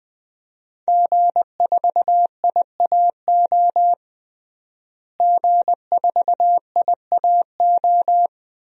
Morse code, heard as a keyed tone, about 21 wpm, 700 Hz. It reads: G4IAO G4IAO
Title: DXCC-100 Call Signs - 5 Characters - Repeated 2x - 20wpm